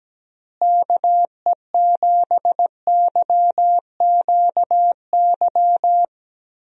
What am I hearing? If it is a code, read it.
Morse code, heard as a keyed tone, about 17 wpm, 700 Hz. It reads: KE7YQY
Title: USA Call Signs - 6 Characters - Repeated 3x - 17wpm